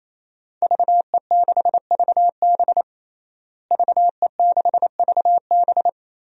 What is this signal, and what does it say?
Morse code, heard as a keyed tone, about 28 wpm, 700 Hz. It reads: VE6VB VE6VB
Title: DXCC-100 Call Signs - 5 Characters - Repeated 3x - 28wpm